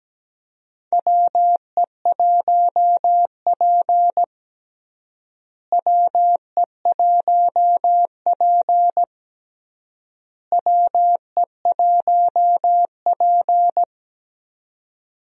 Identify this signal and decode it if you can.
Morse code, heard as a keyed tone, about 17 wpm, 700 Hz. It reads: WE1P WE1P WE1P
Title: USA Call Signs - 4 Characters - Repeated 3x - 17wpm